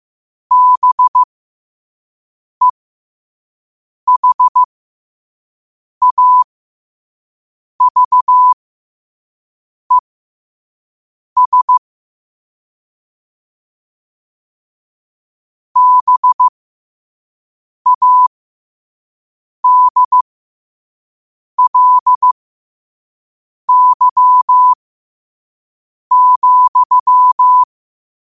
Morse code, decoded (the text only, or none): BEHAVES BADLY,